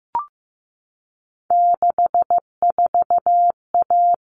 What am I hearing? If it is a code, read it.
Morse code, heard as a keyed tone, about 15 wpm, 700 Hz. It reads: 64A
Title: All Cut Numbers - 3-Digit 15wpm